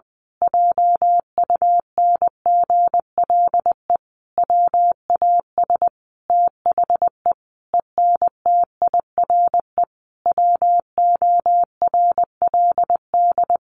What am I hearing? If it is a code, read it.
Morse code, heard as a keyed tone, about 20 wpm, 700 Hz. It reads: JUNGLE WAS THE ENTIRE WORLD
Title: A Sound Of Thunder 20wpm